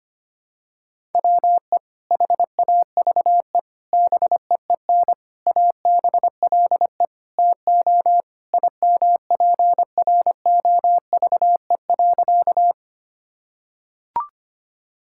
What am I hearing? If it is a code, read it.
Morse code, heard as a keyed tone, about 25 wpm, 700 Hz. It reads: WE HAVE BEEN ABLE TO IMPROVE.
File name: practice / 25wpm / Sentences with 4-Word Phrases - 25wpm